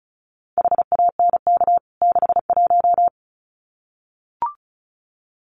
Morse code, heard as a keyed tone, about 34 wpm, 700 Hz. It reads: HANK 61
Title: CWT Exchange 35wpm